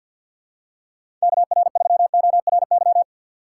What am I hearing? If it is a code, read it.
Morse code, heard as a keyed tone, about 47 wpm, 700 Hz. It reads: KR3ORY
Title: USA Call Signs - 6 Characters - 8x Character Spacing - 50wpm